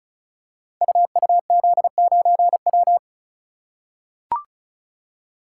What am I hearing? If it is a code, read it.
Morse code, heard as a keyed tone, about 36 wpm, 700 Hz. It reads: UUZ9W
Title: Code Groups - 5 Letter:Numbers - 35wpm